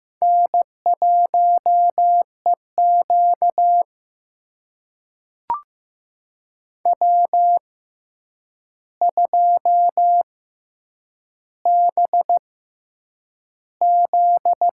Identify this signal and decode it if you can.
Morse code, heard as a keyed tone, about 15 wpm, spaced as for 8 wpm, 700 Hz. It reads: N1EQ W2BZ
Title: USA Call Signs - 4 Characters - 6x Character Spacing - 15wpm